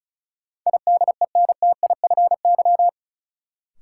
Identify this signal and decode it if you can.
Morse code, heard as a keyed tone, about 36 wpm, 700 Hz. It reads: IDENTIFY
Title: Sentences from Top 900 Words - New Word Drill - 35wpm